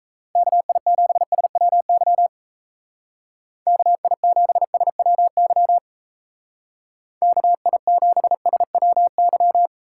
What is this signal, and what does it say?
Morse code, heard as a keyed tone, about 34 wpm, 700 Hz. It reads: KI7SWY KI7SWY KI7SWY
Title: Speed Racing - USA 2x3 Call Signs 28wpm